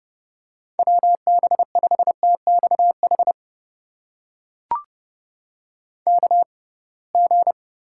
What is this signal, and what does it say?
Morse code, heard as a keyed tone, about 30 wpm, spaced as for 23 wpm, 700 Hz. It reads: WB5TXH KG
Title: USA Call Signs - 6 Characters - 6x Character Spacing - 30wpm